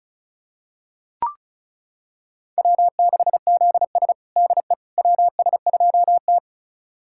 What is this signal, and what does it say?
Morse code, heard as a keyed tone, about 36 wpm, 700 Hz. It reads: W6ZS DE WS2T
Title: USA Callsign DE Callsign - 4 Characters - 35wpm